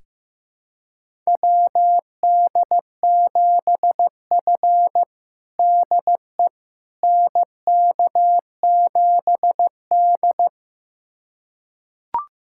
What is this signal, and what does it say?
Morse code, heard as a keyed tone, about 15 wpm, 700 Hz. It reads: WD7F DE NK7D
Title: USA Callsign DE Callsign - 4 Characters - 15wpm